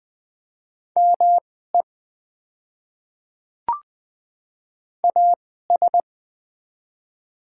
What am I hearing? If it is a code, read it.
Morse code, heard as a keyed tone, about 20 wpm, spaced as for 14 wpm, 700 Hz. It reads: ME AS
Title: 2x Character Spacing - Top 200 Words - 20wpm